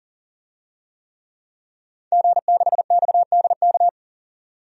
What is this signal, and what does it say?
Morse code, heard as a keyed tone, about 40 wpm, 700 Hz. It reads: G6XDK
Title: DXCC-100 Call Signs - 5 Characters - 40wpm